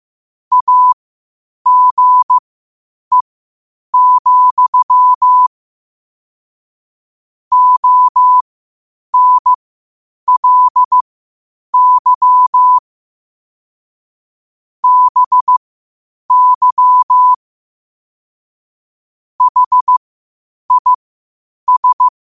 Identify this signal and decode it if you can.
Morse code, heard as a keyed tone, about 15 wpm, spaced as for 8 wpm, 1000 Hz. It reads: AGE, ONLY BY HIS
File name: qotdfiles / 8wpm